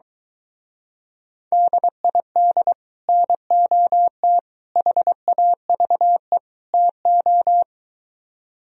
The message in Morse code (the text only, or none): DID NOT HAVE TO